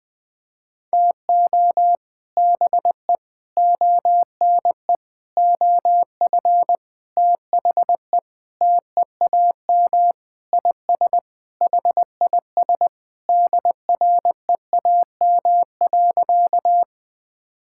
Morse code, heard as a keyed tone, about 20 wpm, 700 Hz. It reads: TO BE ONE OF THE TEAM IS HIS DREAM.
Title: Sentences with 4-Word Phrases - 20wpm